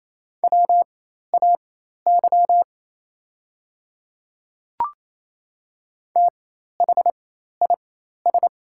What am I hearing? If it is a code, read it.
Morse code, heard as a keyed tone, about 28 wpm, spaced as for 13 wpm, 690 Hz. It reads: WAY THIS
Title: 4x Character Spacing - Top 100 Words - 28wpm